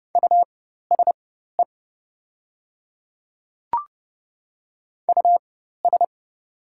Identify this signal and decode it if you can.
Morse code, heard as a keyed tone, about 30 wpm, spaced as for 14 wpm, 700 Hz. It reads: USE US
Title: 4x Character Spacing - Top 300 Words - 30wpm